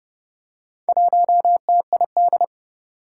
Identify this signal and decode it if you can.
Morse code, heard as a keyed tone, about 31 wpm, 700 Hz. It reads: 1TID
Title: Four Challenging Letter-Number Combinations 30wpm